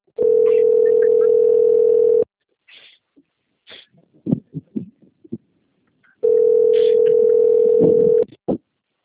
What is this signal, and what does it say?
Ringback tone (North America), 440 + 480 Hz.